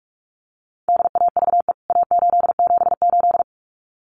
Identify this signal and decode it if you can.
Morse code, heard as a keyed tone, about 44 wpm, 700 Hz. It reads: DAVE A878
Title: CWT Exchange 45wpm